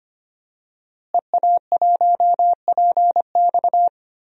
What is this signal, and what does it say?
Morse code, heard as a keyed tone, about 25 wpm, 700 Hz. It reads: EA1PX
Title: DXCC-100 Call Signs - 5 Characters - Repeated 2x - 25wpm